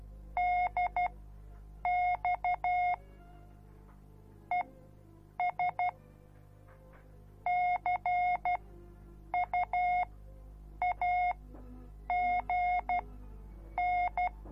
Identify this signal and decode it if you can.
Morse code, heard as a keyed tone, about 12 wpm, spaced as for 7 wpm, 720 Hz. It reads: DX ES CUAGN